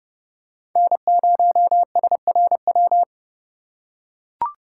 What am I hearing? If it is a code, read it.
Morse code, heard as a keyed tone, about 30 wpm, 700 Hz. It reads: N0SRW
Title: USA Call Signs - 5 Characters - 4x Character Spacing - 30wpm